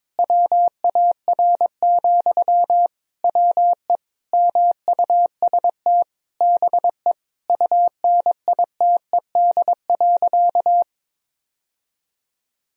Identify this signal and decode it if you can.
Morse code, heard as a keyed tone, about 21 wpm, 700 Hz. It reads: WAR, WE MUST BE UNITED.
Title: Sentences from Top 1000 Words - Review Part 4 - 22wpm